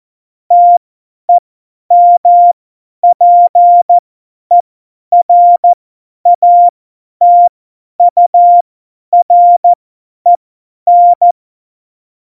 Morse code, heard as a keyed tone, about 14 wpm, spaced as for 10 wpm, 700 Hz. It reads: TEMPERATUREN
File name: Morsekurs / Witze / Witz 22. Oktober 2019, 50 F70